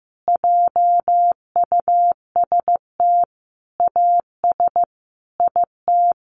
Morse code, heard as a keyed tone, about 15 wpm, 700 Hz. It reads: JUST AS IT
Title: Sentences from Top 700 Words 15wpm